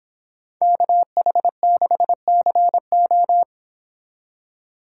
Morse code, heard as a keyed tone, about 29 wpm, 700 Hz. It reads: KH6CO